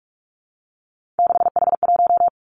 Morse code, heard as a keyed tone, about 46 wpm, 700 Hz. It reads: 6H1
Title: Three Challenging Letter-Number Combinations 45wpm